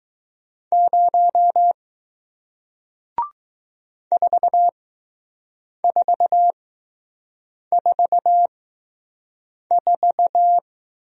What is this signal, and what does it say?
Morse code, heard as a keyed tone, about 20 wpm, spaced as for 6 wpm, 700 Hz. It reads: 0 4444